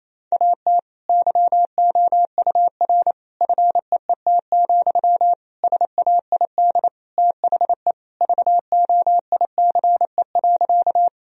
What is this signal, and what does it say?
Morse code, heard as a keyed tone, about 28 wpm, 700 Hz. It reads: AT YOUR FEET, SAID THE VOICE.
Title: Sentences from Top 1000 Words - Review Part 1 - 28wpm